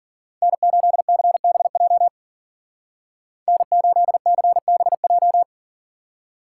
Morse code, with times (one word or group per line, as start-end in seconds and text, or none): 0.4-2.1 N8CBJ
3.5-5.4 N8CBJ